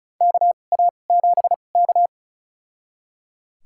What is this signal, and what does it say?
Morse code, heard as a keyed tone, about 36 wpm, spaced as for 25 wpm, 700 Hz. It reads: KA7K
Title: USA Call Signs - 4 Characters - 2x Character Spacing - 35wpm